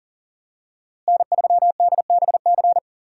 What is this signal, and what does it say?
Morse code, heard as a keyed tone, about 40 wpm, 700 Hz. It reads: N3DBC
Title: USA Call Signs - 5 Characters - Repeated 2x - 40wpm